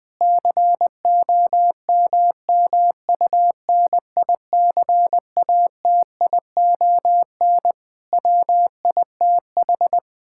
Morse code, heard as a keyed tone, about 20 wpm, 700 Hz. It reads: COMMUNICATION WITH
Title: Sentences from Top 1000 Words - Review Part 6 - 20wpm